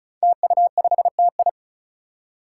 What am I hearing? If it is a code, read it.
Morse code, heard as a keyed tone, about 34 wpm, 700 Hz. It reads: TU5TI
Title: Five Challenging Letter-Number Combinations 35wpm